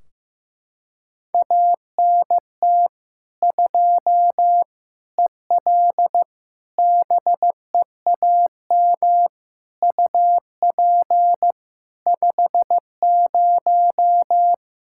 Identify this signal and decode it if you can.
Morse code, heard as a keyed tone, about 15 wpm, 700 Hz. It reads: ANT 2 EL BEAM UP 50
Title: Practice QSO - Exchange 2 - Info - 15wpm